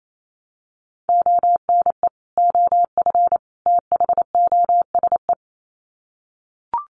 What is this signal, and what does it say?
Morse code, heard as a keyed tone, about 28 wpm, 700 Hz. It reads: ONE OF THOSE
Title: Sets of 3 Words 28wpm